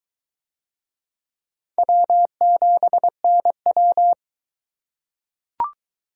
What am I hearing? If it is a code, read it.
Morse code, heard as a keyed tone, about 21 wpm, 700 Hz. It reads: W7NW